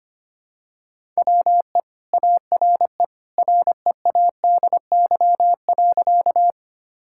Morse code, heard as a keyed tone, about 25 wpm, 700 Hz. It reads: WE ARE READY.
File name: practice / 25wpm / Sentences with 2-Word Phrases - 25wpm